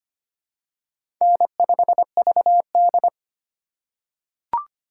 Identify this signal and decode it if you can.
Morse code, heard as a keyed tone, about 25 wpm, 700 Hz. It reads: N5VD
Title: USA Call Signs - 4 Characters - 8x Character Spacing - 25wpm